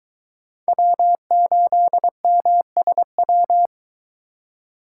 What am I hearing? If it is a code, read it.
Morse code, heard as a keyed tone, about 22 wpm, 700 Hz. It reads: W8MSW